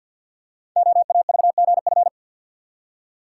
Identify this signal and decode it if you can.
Morse code, heard as a keyed tone, about 46 wpm, 700 Hz. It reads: OAVGF
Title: Code Groups - 5 Letters - 50wpm